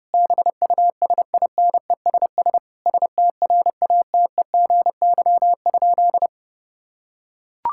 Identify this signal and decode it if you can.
Morse code, heard as a keyed tone, about 30 wpm, 700 Hz. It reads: BUSINESS STRATEGY?
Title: Sentences from Top 1000 Words - Review Part 5 - 30wpm